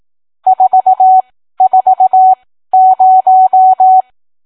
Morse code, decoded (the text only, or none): 440